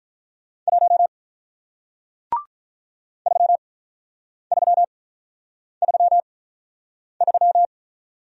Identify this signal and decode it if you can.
Morse code, heard as a keyed tone, about 43 wpm, spaced as for 8 wpm, 700 Hz. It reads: 1 3333
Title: Speed-Racing Single Letter-Number 35wpm